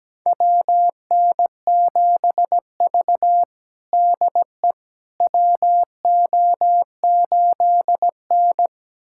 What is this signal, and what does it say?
Morse code, heard as a keyed tone, about 17 wpm, 700 Hz. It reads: WN7V DE WO8N
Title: USA Callsign DE Callsign - 4 Characters - 17wpm